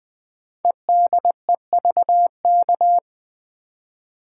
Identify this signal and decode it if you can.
Morse code, heard as a keyed tone, about 20 wpm, 700 Hz. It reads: EDEVK
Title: Code Groups - 5 Letters - 20wpm